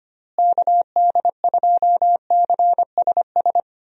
Morse code, heard as a keyed tone, about 25 wpm, 700 Hz. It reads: KD2CSS
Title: USA Call Signs - 6 Characters - Repeated 2x - 25wpm